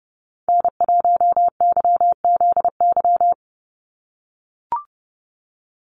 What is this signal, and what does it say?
Morse code, heard as a keyed tone, about 30 wpm, 700 Hz. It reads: N1YZY
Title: USA Call Signs 30wpm